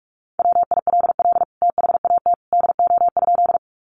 Morse code, heard as a keyed tone, about 44 wpm, 700 Hz. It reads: WILL THAT DO?
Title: Sentences from Top 100 Words 45wpm